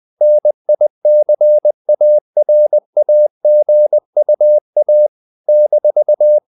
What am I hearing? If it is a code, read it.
Morse code, heard as a keyed tone, about 20 wpm, 600 Hz. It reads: NICARAGUA -
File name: CW / Practice / NationalCapitals20